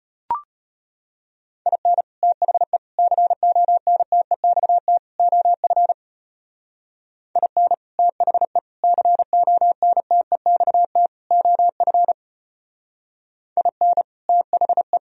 Morse code, heard as a keyed tone, about 31 wpm, 700 Hz. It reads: IN THE CONTEXT OF IN THE CONTEXT OF IN THE
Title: Speed-Racing - Sets of 4 Words 25wpm